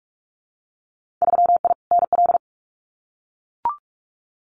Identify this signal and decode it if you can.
Morse code, heard as a keyed tone, about 44 wpm, 700 Hz. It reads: 3I NL
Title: Winter Field Day Exchanges 45wpm